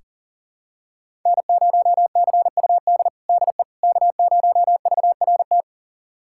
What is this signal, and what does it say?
Morse code, heard as a keyed tone, about 40 wpm, 700 Hz. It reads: N0CUD DE K0VRT
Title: USA Callsign DE Callsign - 5 Characters - 40wpm